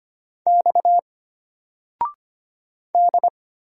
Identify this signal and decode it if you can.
Morse code, heard as a keyed tone, about 25 wpm, 700 Hz. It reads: X D
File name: Single-Letter-Number-25wpm